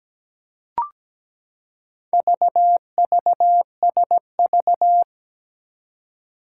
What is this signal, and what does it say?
Morse code, heard as a keyed tone, about 17 wpm, 700 Hz. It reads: VVSV
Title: Four Challenging Letter-Number Combinations 17wpm